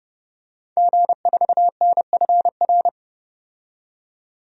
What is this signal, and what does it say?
Morse code, heard as a keyed tone, about 31 wpm, 700 Hz. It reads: G4NFR
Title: DXCC-100 Call Signs - 5 Characters - 30wpm